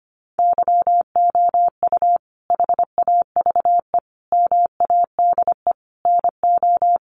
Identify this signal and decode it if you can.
Morse code, heard as a keyed tone, about 25 wpm, 700 Hz. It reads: YOU HAVE MADE NO
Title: Sentences from Top 1400 Words 25wpm